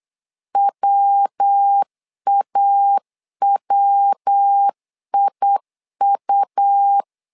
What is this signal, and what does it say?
Morse code, heard as a keyed tone, about 8 wpm, 800 Hz. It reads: WAWIU
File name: LEZIONI / 3 MWRO / 3 MW 8wpm 2501